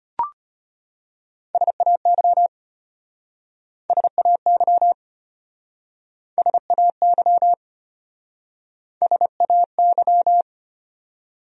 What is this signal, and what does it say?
Morse code, heard as a keyed tone, about 33 wpm, 700 Hz. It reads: SAY SAY SAY SAY